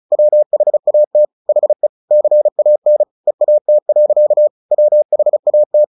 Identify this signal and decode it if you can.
Morse code, heard as a keyed tone, about 35 wpm, 600 Hz. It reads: WHAT HE CAN EAT. WHAT
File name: the old man and the sea 2 35wpm